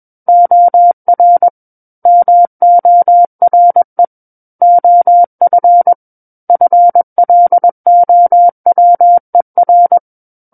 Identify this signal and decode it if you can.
Morse code, heard as a keyed tone, about 21 wpm, 700 Hz. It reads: OR MORE OF FLOWER
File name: Book One / 12-Chapter 12 - 21 WPM